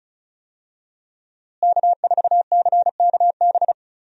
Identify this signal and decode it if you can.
Morse code, heard as a keyed tone, about 36 wpm, 700 Hz. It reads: K4CKB
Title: USA Call Signs - 5 Characters - Repeated 3x - 35wpm